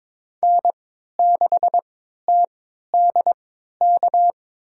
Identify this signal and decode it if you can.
Morse code, heard as a keyed tone, about 22 wpm, spaced as for 12 wpm, 700 Hz. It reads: N6TDK
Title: USA Call Signs - 5 Characters - 3x Character Spacing - 22wpm